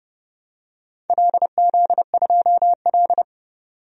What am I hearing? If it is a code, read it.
Morse code, heard as a keyed tone, about 30 wpm, 700 Hz. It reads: LZ2L
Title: DXCC-100 Call Signs - 4 Characters - 30wpm